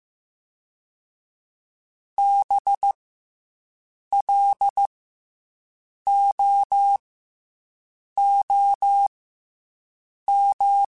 Morse code, heard as a keyed tone, about 15 wpm, 800 Hz. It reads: B L O O M
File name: Special INGs / Special INGs-mp3s / SlowToFaster ing @13